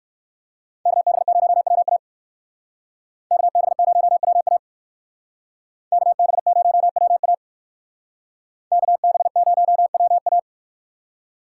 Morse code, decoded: KB0WA KB0WA KB0WA KB0WA

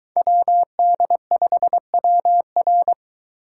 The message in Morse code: WD5WR

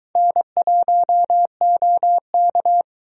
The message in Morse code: N1OK

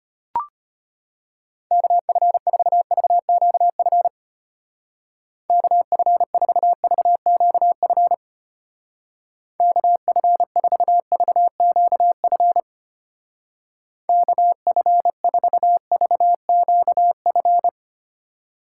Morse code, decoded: KF4VQF KF4VQF KF4VQF KF4VQF

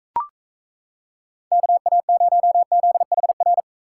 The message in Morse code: KA0ZHR

